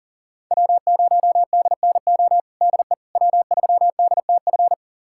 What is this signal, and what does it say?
Morse code, heard as a keyed tone, about 37 wpm, 700 Hz. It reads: W0DNO DE W3DTF
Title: USA Callsign DE Callsign - 5 Characters - 40wpm